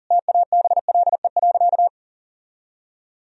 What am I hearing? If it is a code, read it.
Morse code, heard as a keyed tone, about 40 wpm, 700 Hz. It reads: TABLE.
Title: Sentences from Top 1000 Words - Review Part 4 - 40wpm